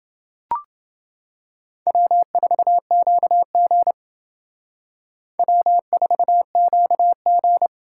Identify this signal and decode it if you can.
Morse code, heard as a keyed tone, about 28 wpm, 700 Hz. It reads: W4QG W4QG